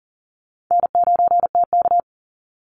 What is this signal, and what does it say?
Morse code, heard as a keyed tone, about 40 wpm, 700 Hz. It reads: N9TK
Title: Easy Call Signs - 40wpm